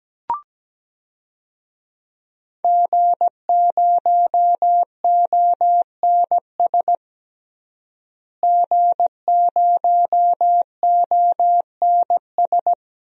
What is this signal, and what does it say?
Morse code, heard as a keyed tone, about 17 wpm, 700 Hz. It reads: G0ONS G0ONS
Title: DXCC-100 Call Signs - 5 Characters - Repeated 3x - 17wpm